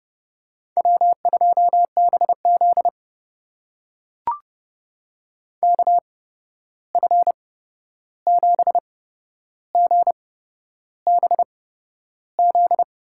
Morse code, heard as a keyed tone, about 30 wpm, spaced as for 11 wpm, 700 Hz. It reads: W2BZ KF7GBZ
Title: USA Call Signs - 8x Character Spacing - 30wpm